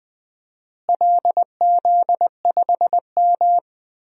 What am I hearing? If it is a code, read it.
Morse code, heard as a keyed tone, about 21 wpm, 700 Hz. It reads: LZ5M
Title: DXCC-100 Call Signs - 4 Characters - Repeated 3x - 20wpm